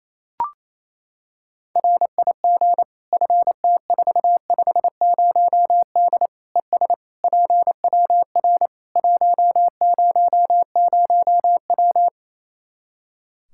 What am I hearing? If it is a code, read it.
Morse code, heard as a keyed tone, about 28 wpm, 700 Hz. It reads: RIG FT450D ES PWR 100W
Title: Practice QSO - Exchange 2 - Info - 28wpm